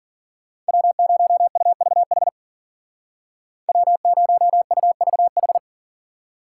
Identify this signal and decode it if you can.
Morse code, heard as a keyed tone, about 52 wpm, 700 Hz. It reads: W0UVH W0UVH